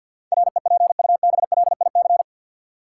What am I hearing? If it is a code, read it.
Morse code, heard as a keyed tone, about 52 wpm, 700 Hz. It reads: REPUBLIC
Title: Sentences from Top 1500 Words - New Word Drill Alt - 50wpm